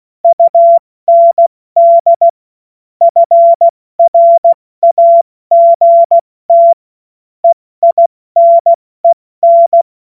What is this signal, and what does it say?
Morse code, heard as a keyed tone, about 16 wpm, spaced as for 14 wpm, 680 Hz. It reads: UND FRAGT EINEN
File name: Morsekurs / Witze / Witz 24 Juni Mai 2025 70 F80